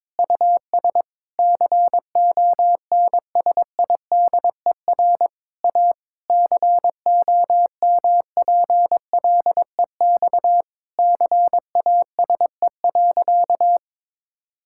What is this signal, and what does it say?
Morse code, heard as a keyed tone, about 22 wpm, 700 Hz. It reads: US CONSIDER A COMPLEX CASE.
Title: Sentences from Top 1000 Words - Review Part 6 - 22wpm